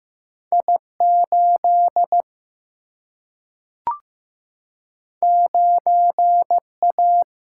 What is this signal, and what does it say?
Morse code, heard as a keyed tone, about 15 wpm, 700 Hz. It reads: I 8 9 A
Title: Single-Letter Single-Number 15wpm